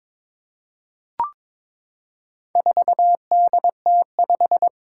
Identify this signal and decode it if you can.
Morse code, heard as a keyed tone, about 21 wpm, 700 Hz. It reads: 4DT5
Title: Four Challenging Letter-Number Combinations 22wpm